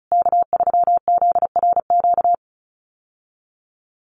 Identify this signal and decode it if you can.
Morse code, heard as a keyed tone, about 34 wpm, 700 Hz. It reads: K3ZRQ
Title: USA Call Signs Repeated 3x 35wpm